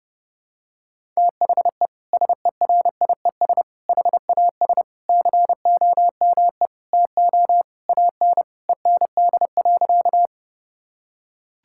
Morse code, heard as a keyed tone, about 32 wpm, 700 Hz. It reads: THE SERIES HAS COME TO AN END.
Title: Sentences from Top 1000 Words - Review Part 5 - 30wpm